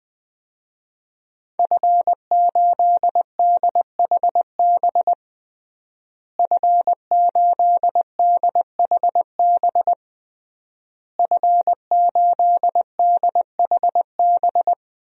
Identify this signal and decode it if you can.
Morse code, heard as a keyed tone, about 20 wpm, 700 Hz. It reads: F8DHB F8DHB F8DHB
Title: DXCC-100 Call Signs - 5 Characters - Repeated 3x - 20wpm